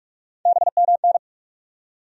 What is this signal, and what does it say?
Morse code, heard as a keyed tone, about 46 wpm, 700 Hz. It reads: BMN
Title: USA Call Sign Postfixes - 3 Letters 45wpm